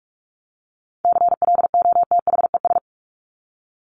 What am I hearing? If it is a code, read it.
Morse code, heard as a keyed tone, about 44 wpm, 700 Hz. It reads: CLOTHES
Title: Top 1751-2000 Words 45wpm